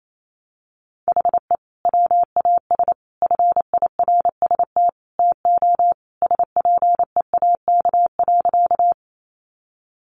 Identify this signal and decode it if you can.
Morse code, heard as a keyed tone, about 28 wpm, 700 Hz. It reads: HE WAS FIRST TO SPEAK.
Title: Sentences from Top 1100 Words 28wpm